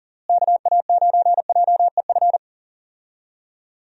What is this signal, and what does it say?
Morse code, heard as a keyed tone, about 40 wpm, 700 Hz. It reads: KA9JEF
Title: USA Call Signs - 6 Characters - Repeated 2x - 40wpm